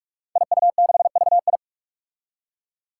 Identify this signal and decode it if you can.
Morse code, heard as a keyed tone, about 45 wpm, 700 Hz. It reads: IU6VI
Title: Five Challenging Letter-Number Combinations 45wpm